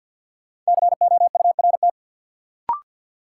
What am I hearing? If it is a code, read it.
Morse code, heard as a keyed tone, about 53 wpm, 700 Hz. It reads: COURT
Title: Sentences from Top 1600 Words - New Word Drill Alt - 50wpm